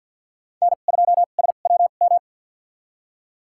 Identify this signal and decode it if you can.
Morse code, heard as a keyed tone, about 53 wpm, spaced as for 37 wpm, 700 Hz. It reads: N2SWM
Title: USA Call Signs - 5 Characters - 2x Character Spacing - 50wpm